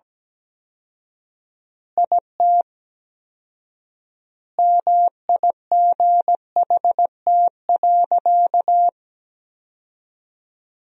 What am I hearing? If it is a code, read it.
Morse code, heard as a keyed tone, about 17 wpm, 700 Hz. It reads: IT MIGHT.
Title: 4x Word-Spacing - Sentences from Top 500 Words - 17wpm